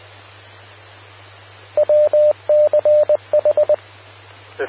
Morse code, heard as a keyed tone, about 20 wpm, 600 Hz. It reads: WCH
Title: Austin, TX Amateur Repeaters